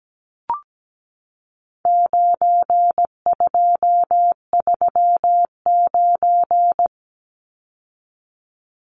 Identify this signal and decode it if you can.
Morse code, heard as a keyed tone, about 17 wpm, 700 Hz. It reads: 9239
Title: Numbers - 4-Digit 17wpm